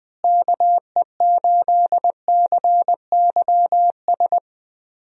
Morse code, heard as a keyed tone, about 20 wpm, 700 Hz. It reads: KE8CYS
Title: USA Call Signs - 6 Characters - Repeated 2x - 20wpm